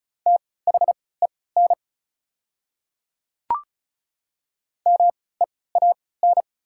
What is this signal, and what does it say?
Morse code, heard as a keyed tone, about 36 wpm, spaced as for 20 wpm, 700 Hz. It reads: THEN MEAN
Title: 3x Character Spacing - Top 300 Words - 35wpm